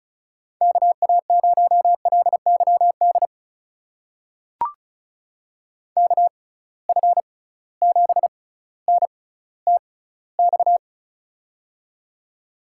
Morse code, heard as a keyed tone, about 36 wpm, spaced as for 18 wpm, 700 Hz. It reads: KA0LYD KF7NTX
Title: USA Call Signs - 6 Characters - 6x Character Spacing - 35wpm